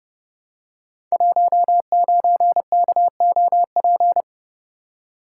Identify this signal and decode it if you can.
Morse code, heard as a keyed tone, about 30 wpm, 700 Hz. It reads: 19KOP
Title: Code Groups - 5 Letter:Numbers - 30wpm